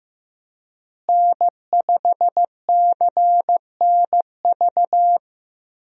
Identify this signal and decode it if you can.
Morse code, heard as a keyed tone, about 15 wpm, 700 Hz. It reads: N5CNV
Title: USA Call Signs - 5 Characters - Repeated 3x - 15wpm